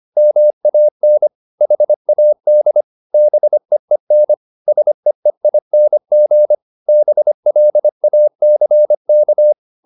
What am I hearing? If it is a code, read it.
Morse code, heard as a keyed tone, about 25 wpm, 600 Hz. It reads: MAN HAD BEEN SEEING BLACK